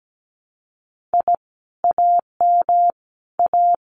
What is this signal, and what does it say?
Morse code, heard as a keyed tone, about 17 wpm, 700 Hz. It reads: I AM A
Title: Common QSO Phrases Encore 17wpm